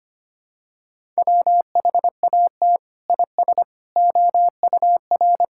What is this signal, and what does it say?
Morse code, heard as a keyed tone, about 25 wpm, 700 Hz. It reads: WHAT IS OUR
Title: Sentences from Top 1000 Words - Review Part 5 - 25wpm